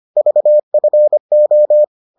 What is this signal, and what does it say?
Morse code, heard as a keyed tone, about 25 wpm, 600 Hz. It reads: VFO